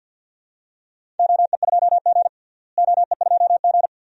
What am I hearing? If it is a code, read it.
Morse code, heard as a keyed tone, about 53 wpm, 700 Hz. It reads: OE2G OE2G
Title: DXCC-100 Call Signs - 4 Characters - Repeated 3x - 50wpm